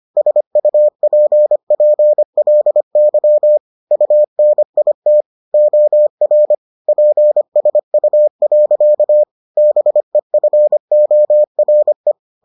Morse code, heard as a keyed tone, about 25 wpm, 600 Hz. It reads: SUPPLY UNIT OR PSU. BEFORE